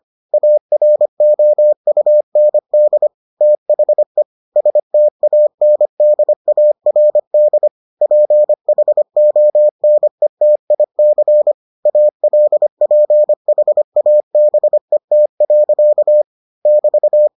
Morse code, heard as a keyed tone, about 25 wpm, 600 Hz. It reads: AROUND THE STANDARD PHONETIC ALPHABET. <BT>